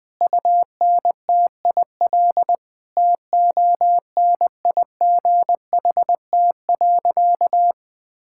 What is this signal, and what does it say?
Morse code, heard as a keyed tone, about 21 wpm, 700 Hz. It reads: UNTIL TONIGHT.
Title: Sentences from Top 1000 Words - Review Part 6 - 20wpm